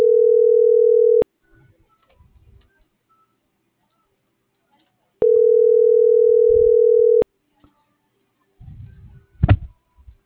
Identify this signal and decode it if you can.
Ringback tone (North America), 440 + 480 Hz.